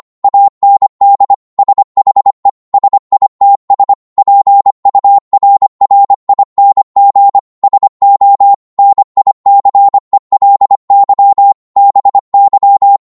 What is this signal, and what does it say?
Morse code, heard as a keyed tone, about 25 wpm, 800 Hz. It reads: AND SHE SITS PURRING SO NICELY BY